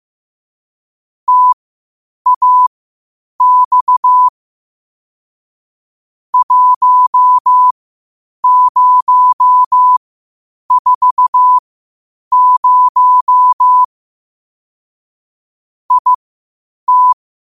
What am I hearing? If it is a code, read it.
Morse code, heard as a keyed tone, about 15 wpm, spaced as for 8 wpm, 1000 Hz. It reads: TAX 1040 IT
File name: qotdfiles / 8wpm